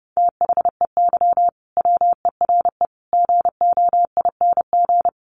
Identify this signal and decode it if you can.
Morse code, heard as a keyed tone, about 30 wpm, 700 Hz. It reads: THEY WERE GOING
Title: Sets of 4 Words 30wpm